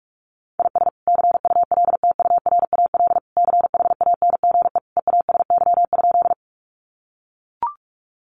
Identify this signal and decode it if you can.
Morse code, heard as a keyed tone, about 44 wpm, 700 Hz. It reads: IS CULTURAL CHANGE EASY?
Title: Sentences from Top 1500 Words 45wpm